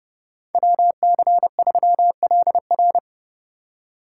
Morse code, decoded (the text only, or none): WC3LR